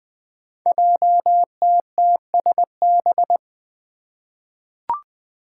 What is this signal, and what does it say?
Morse code, heard as a keyed tone, about 20 wpm, 700 Hz. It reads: JTTSB
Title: Five Challenging Letter-Number Combinations 20wpm